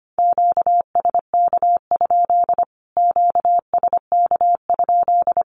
Morse code, heard as a keyed tone, about 25 wpm, 700 Hz. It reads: QSK? QSK?
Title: Q-Codes Repeated 3x 25wpm